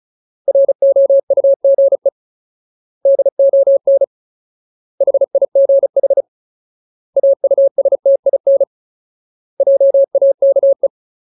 Morse code, heard as a keyed tone, about 36 wpm, 550 Hz. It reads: ROUGE DON HIGH AUSTIN JAKE
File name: Ultimate Word List / 35WPM Ultimate